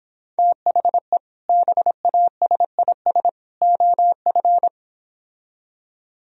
Morse code, heard as a keyed tone, about 29 wpm, 700 Hz. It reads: THE BASIS OF